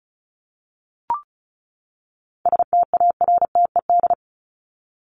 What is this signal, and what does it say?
Morse code, heard as a keyed tone, about 34 wpm, 700 Hz. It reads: STARTED
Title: ED Suffix - Top 100 Words with Base Words 35wpm